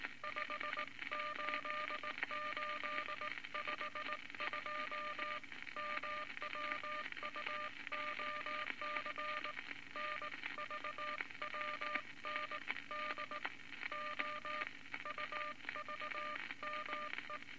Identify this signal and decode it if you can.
Morse code, heard as a keyed tone, about 19 wpm, 1200 Hz, with noise. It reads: 58852 MWUOC NVWND OUVME
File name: Morse Code Records / 19 B LCM 167 Twelve Words Per Minute